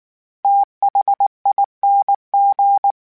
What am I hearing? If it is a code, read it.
Morse code, heard as a keyed tone, about 19 wpm, 800 Hz. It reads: THING
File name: Special INGs / Special INGs-mp3s / VerySlowToFaster Ing @18